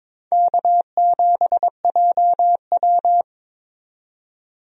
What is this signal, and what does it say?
Morse code, heard as a keyed tone, about 22 wpm, 700 Hz. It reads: K7JW
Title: USA Call Signs - 4 Characters - 22wpm